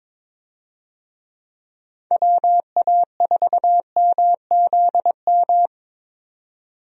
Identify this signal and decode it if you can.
Morse code, heard as a keyed tone, about 22 wpm, 700 Hz. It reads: WA4MZM
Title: USA Call Signs - 6 Characters - 22wpm